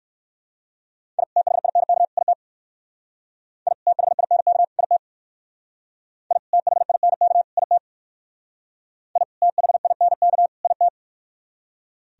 Morse code, heard as a keyed tone, about 53 wpm, 700 Hz. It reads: I THINK IT I THINK IT I THINK IT I THINK IT
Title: Speed-Racing - Sets of 3 Words 45wpm